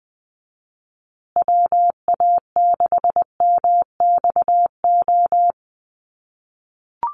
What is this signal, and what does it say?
Morse code, heard as a keyed tone, about 20 wpm, 700 Hz. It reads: WA6MXO
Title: USA Call Signs Repeated 2x 20wpm